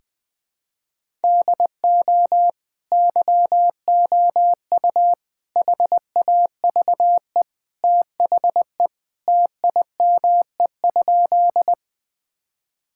Morse code, heard as a keyed tone, about 21 wpm, 700 Hz. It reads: DO YOU HAVE THE TIME?
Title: Sentences with 2-Word Phrases - 20wpm